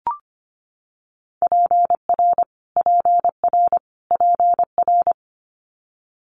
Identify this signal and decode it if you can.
Morse code, heard as a keyed tone, about 25 wpm, 700 Hz. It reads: PR PR PR
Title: Multi-letter Phonemes Repeated 3x 25wpm